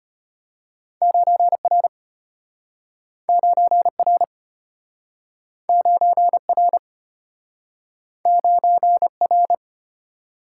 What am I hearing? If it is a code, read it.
Morse code, heard as a keyed tone, about 34 wpm, 700 Hz. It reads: 9R 9R 9R 9R